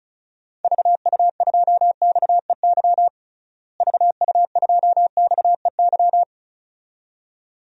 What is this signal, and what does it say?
Morse code, heard as a keyed tone, about 34 wpm, 700 Hz. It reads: VU2XEY VU2XEY
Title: DXCC-100 Call Signs - 6 Characters - Repeated 2x - 35wpm